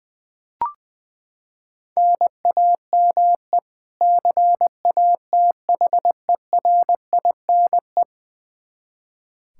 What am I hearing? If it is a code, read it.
Morse code, heard as a keyed tone, about 21 wpm, 700 Hz. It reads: NAME CATHERINE
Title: Practice QSO - Exchange 1 - Beginning - 20wpm